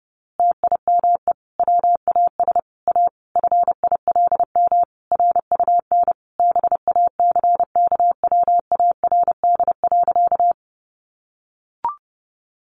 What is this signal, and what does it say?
Morse code, heard as a keyed tone, about 30 wpm, 700 Hz. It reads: TIME WAS A FILM RUN BACKWARD.
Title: A Sound Of Thunder 30wpm